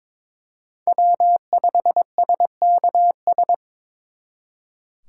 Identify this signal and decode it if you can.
Morse code, heard as a keyed tone, about 26 wpm, 700 Hz. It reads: W5SKS
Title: Speed Racing - USA Call Signs 22wpm